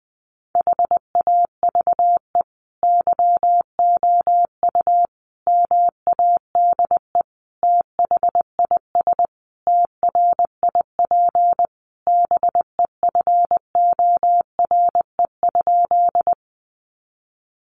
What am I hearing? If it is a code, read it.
Morse code, heard as a keyed tone, about 20 wpm, 700 Hz. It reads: HAVE YOU MADE THIS TRIP BEFORE?
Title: Sentences from Top 1300 Words 20wpm